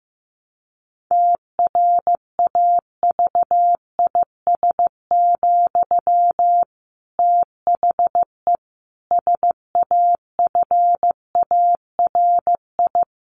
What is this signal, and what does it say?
Morse code, heard as a keyed tone, about 15 wpm, 700 Hz. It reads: TRAVIS, THE SAFARI
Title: A Sound Of Thunder 15wpm